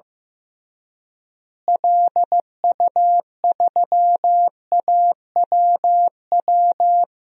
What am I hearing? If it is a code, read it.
Morse code, heard as a keyed tone, about 16 wpm, 700 Hz. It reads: LU3AWW
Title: DXCC-100 Call Signs - 6 Characters - Repeated 2x - 15wpm